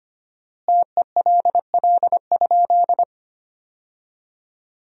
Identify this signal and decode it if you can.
Morse code, heard as a keyed tone, about 25 wpm, 700 Hz. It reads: TELL?
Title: 3x Word-Spacing - Sentences from Top 200 Words - 25wpm